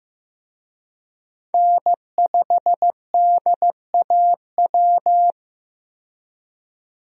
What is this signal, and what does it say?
Morse code, heard as a keyed tone, about 15 wpm, 700 Hz. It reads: N5DAW
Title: USA Call Signs - 5 Characters - 15wpm